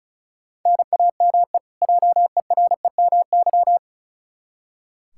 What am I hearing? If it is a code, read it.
Morse code, heard as a keyed tone, about 36 wpm, 700 Hz. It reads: NAME JEREMY
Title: K1USN SST QSO Practice 35wpm